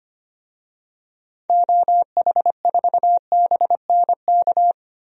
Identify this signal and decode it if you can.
Morse code, heard as a keyed tone, about 25 wpm, 700 Hz. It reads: OH4BNK
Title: DXCC-100 Call Signs - 6 Characters - 25wpm